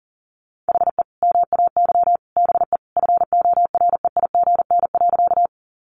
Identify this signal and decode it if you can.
Morse code, heard as a keyed tone, about 40 wpm, 700 Hz. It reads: HE MAY BE FOREIGN.
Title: Sentences from Top 1400 Words 40wpm